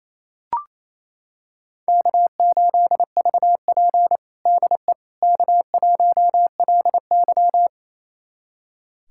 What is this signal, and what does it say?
Morse code, heard as a keyed tone, about 28 wpm, 700 Hz. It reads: K8VP DE K1LY
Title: USA Callsign DE Callsign - 4 Characters - 28wpm